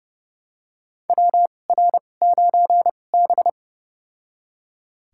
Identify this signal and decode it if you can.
Morse code, heard as a keyed tone, about 31 wpm, spaced as for 22 wpm, 700 Hz. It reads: WR9B
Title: USA Call Signs - 2x Character Spacing - 30wpm